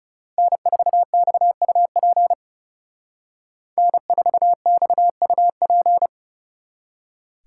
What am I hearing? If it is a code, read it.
Morse code, heard as a keyed tone, about 39 wpm, 700 Hz. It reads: N4XUP N4XUP